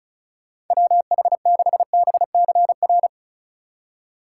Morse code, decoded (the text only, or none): WH6BCR